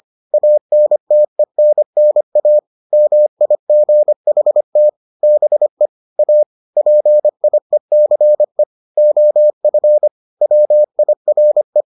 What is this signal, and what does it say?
Morse code, heard as a keyed tone, about 25 wpm, 600 Hz. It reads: ANTENNA MIGHT BE A PIECE OF WIRE